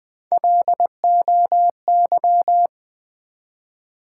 Morse code, heard as a keyed tone, about 20 wpm, 700 Hz. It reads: LOY
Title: Build A Word - 1 Letter At A Time - 3 to 5-Letter Words - 20wpm